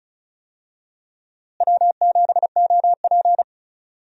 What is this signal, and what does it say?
Morse code, heard as a keyed tone, about 36 wpm, 700 Hz. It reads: W7OP
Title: USA Call Signs - 4 Characters - Repeated 3x - 35wpm